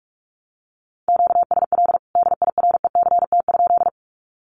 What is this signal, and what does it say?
Morse code, heard as a keyed tone, about 45 wpm, 700 Hz. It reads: QSL DIRECT?
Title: Common QSO Phrases 45wpm